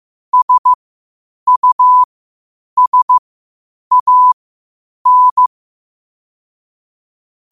Morse code, heard as a keyed tone, about 15 wpm, spaced as for 8 wpm, 1000 Hz. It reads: SUSAN
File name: qotdfiles / 8wpm